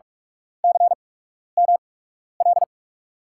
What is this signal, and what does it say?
Morse code, heard as a keyed tone, about 44 wpm, spaced as for 12 wpm, 700 Hz. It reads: CML
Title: USA Call Signs - 5 Characters - 8x Character Spacing - 45wpm